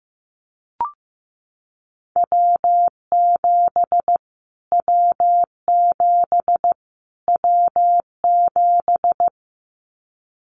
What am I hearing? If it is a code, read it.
Morse code, heard as a keyed tone, about 15 wpm, 700 Hz. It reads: W7 W7 W7
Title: Callsign Prefixes for Top 25 DXCC - Repeated 3x - 15wpm